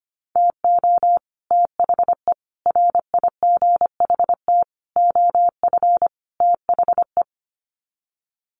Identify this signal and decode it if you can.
Morse code, heard as a keyed tone, about 25 wpm, 700 Hz. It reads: TO THE RIGHT OF THE
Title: Sets of 5 Words Encore 25wpm